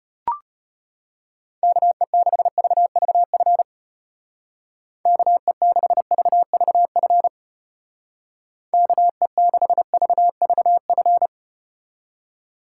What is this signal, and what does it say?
Morse code, heard as a keyed tone, about 33 wpm, 700 Hz. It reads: KE6VVF KE6VVF KE6VVF